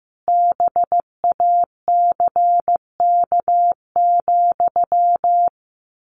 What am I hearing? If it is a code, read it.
Morse code, heard as a keyed tone, about 15 wpm, 700 Hz. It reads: BACK,
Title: A Sound Of Thunder 15wpm